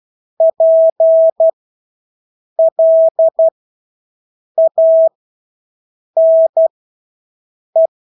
Morse code, heard as a keyed tone, about 12 wpm, spaced as for 6 wpm, 650 Hz. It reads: PLANE
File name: Engelske ord 12-6 wpm